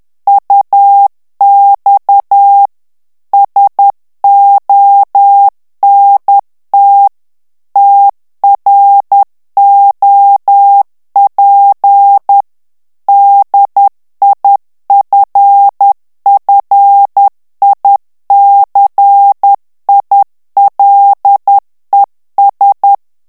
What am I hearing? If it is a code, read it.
Morse code, heard as a keyed tone, about 10 wpm, 800 Hz. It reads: UX SONT TROP DIFFICILES
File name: morse mp3 / FRANCAIS TEXTE CLAIR 2 VITESSE 10 MOTS MINUTES